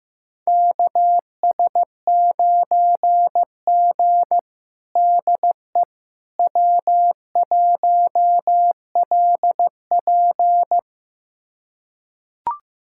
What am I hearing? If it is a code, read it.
Morse code, heard as a keyed tone, about 15 wpm, 700 Hz. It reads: KS9G DE W1LP
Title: USA Callsign DE Callsign - 4 Characters - 15wpm